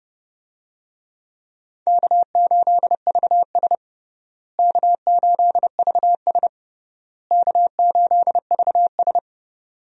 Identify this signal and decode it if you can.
Morse code, heard as a keyed tone, about 30 wpm, 700 Hz. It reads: K8VS K8VS K8VS
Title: USA Call Signs - 4 Characters - Repeated 3x - 30wpm